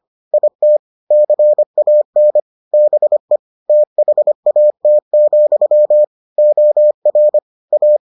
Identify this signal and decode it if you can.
Morse code, heard as a keyed tone, about 25 wpm, 600 Hz. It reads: IT CAN BE THAT, OR A